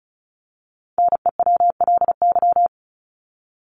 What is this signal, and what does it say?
Morse code, heard as a keyed tone, about 34 wpm, 700 Hz. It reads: NEWLY
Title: LY Suffix - Top 100 Words with Base Words 35wpm